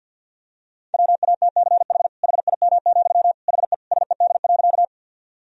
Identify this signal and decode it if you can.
Morse code, heard as a keyed tone, about 50 wpm, 700 Hz. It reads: WATCH HIM, HE SED.
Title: Sentences from Top 300 Words and Abbreviations 50wpm